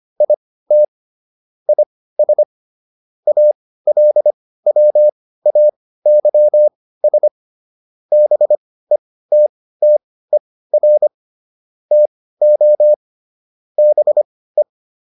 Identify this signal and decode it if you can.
Morse code, heard as a keyed tone, about 25 wpm, spaced as for 16 wpm, 600 Hz. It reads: IT IS ALWAYS BETTER TO BE